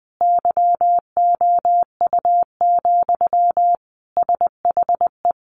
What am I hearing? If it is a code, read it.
Morse code, heard as a keyed tone, about 20 wpm, 700 Hz. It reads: YOU, SHE
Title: Sentences from Top 400 Words 20wpm